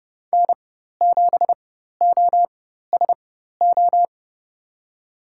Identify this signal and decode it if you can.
Morse code, heard as a keyed tone, about 30 wpm, spaced as for 14 wpm, 700 Hz. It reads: N7OSO
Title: USA Call Signs - 5 Characters - 4x Character Spacing - 30wpm